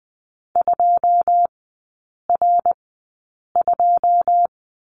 Morse code, heard as a keyed tone, about 20 wpm, spaced as for 8 wpm, 700 Hz. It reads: 2R2
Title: 081 Lesson 14e - Number 2 - Three Characters 20wpm